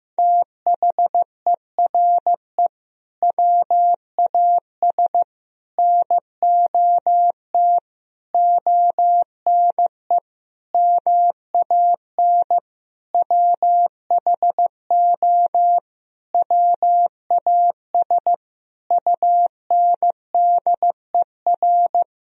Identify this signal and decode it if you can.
Morse code, heard as a keyed tone, about 16 wpm, 700 Hz. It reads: THERE WAS NOT ONE MAN WHO WAS UNDER
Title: Sentences from Top 1000 Words - Review Part 2 - 15wpm